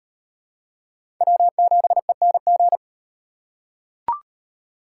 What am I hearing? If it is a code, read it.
Morse code, heard as a keyed tone, about 35 wpm, 700 Hz. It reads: W7ENG